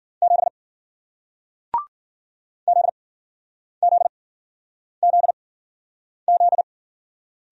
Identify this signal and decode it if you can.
Morse code, heard as a keyed tone, about 53 wpm, spaced as for 9 wpm, 700 Hz. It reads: 7 ZZZZ